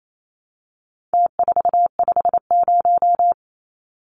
Morse code, heard as a keyed tone, about 28 wpm, 700 Hz. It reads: T450
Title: Numbers - Contest Style - 4-Digit 28wpm